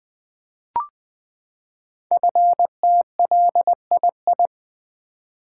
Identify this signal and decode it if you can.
Morse code, heard as a keyed tone, about 20 wpm, 700 Hz. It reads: FTLII
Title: Code Groups - 5 Letters - 20wpm